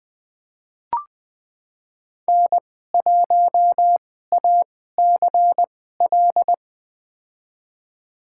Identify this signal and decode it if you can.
Morse code, heard as a keyed tone, about 20 wpm, spaced as for 14 wpm, 700 Hz. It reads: N1ACL
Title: USA Call Signs - 2x Character Spacing - 20wpm